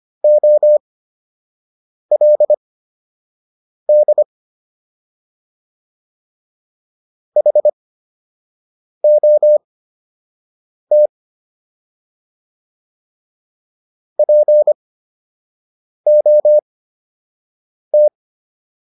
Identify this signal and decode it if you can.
Morse code, heard as a keyed tone, about 25 wpm, spaced as for 6 wpm, 600 Hz. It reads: OLD HOT POT